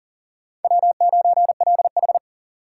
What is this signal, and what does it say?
Morse code, heard as a keyed tone, about 40 wpm, 690 Hz. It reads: W9LH
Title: USA Call Signs - 4x Character Spacing - 40wpm